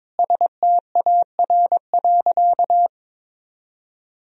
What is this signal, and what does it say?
Morse code, heard as a keyed tone, about 21 wpm, 700 Hz. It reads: STAR.
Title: Sentences from Top 1000 Words - Review Part 2 - 22wpm